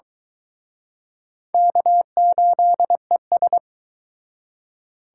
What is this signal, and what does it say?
Morse code, heard as a keyed tone, about 23 wpm, 700 Hz. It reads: K8ES